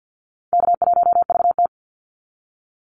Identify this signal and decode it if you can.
Morse code, heard as a keyed tone, about 48 wpm, 700 Hz. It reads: K1VT
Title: Easy Call Signs Repeated 2x - 50wpm